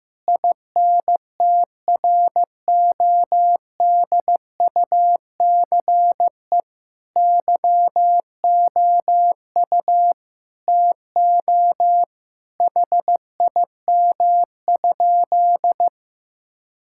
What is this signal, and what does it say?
Morse code, heard as a keyed tone, about 16 wpm, 700 Hz. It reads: INTRODUCE YOU TO HIM?
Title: Sentences from Top 1600 Words 15wpm